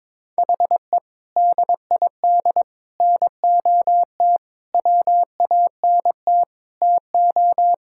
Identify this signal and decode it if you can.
Morse code, heard as a keyed tone, about 22 wpm, 700 Hz. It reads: HE DID NOT WANT TO
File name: practice / 22wpm / Sentences with 4-Word Phrases - 22wpm